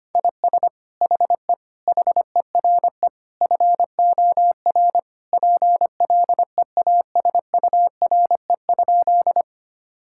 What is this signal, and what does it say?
Morse code, heard as a keyed tone, about 25 wpm, 700 Hz. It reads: IS HE HERE FOR PLEASURE?
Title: Sentences from Top 1600 Words 25wpm